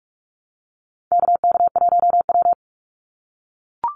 Morse code, heard as a keyed tone, about 44 wpm, 700 Hz. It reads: KK1W
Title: NA SOTA Chaser Callsigns 45wpm